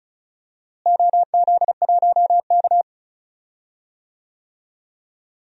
Morse code, heard as a keyed tone, about 36 wpm, 700 Hz. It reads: OZ1K
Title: DXCC-100 Call Signs - 4 Characters - Repeated 2x - 35wpm